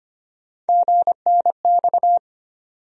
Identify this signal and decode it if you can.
Morse code, heard as a keyed tone, about 25 wpm, 700 Hz. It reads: GNX
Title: USA Call Sign Postfixes - 3 Letters 25wpm